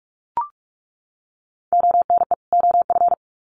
Morse code, heard as a keyed tone, about 44 wpm, 700 Hz. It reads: ONE OF
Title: Sets of 2 Words 45wpm